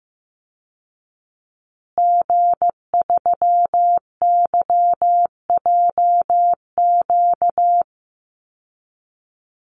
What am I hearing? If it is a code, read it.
Morse code, heard as a keyed tone, about 15 wpm, 700 Hz. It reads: G3YJQ
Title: Call Signs Repeated 2x - 15wpm